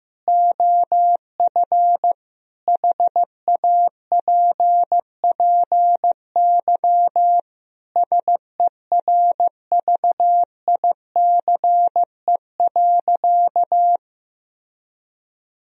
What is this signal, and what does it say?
Morse code, heard as a keyed tone, about 16 wpm, 700 Hz. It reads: OF HAPPY SERVICE.
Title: Sentences from Top 1000 Words - Review Part 4 - 15wpm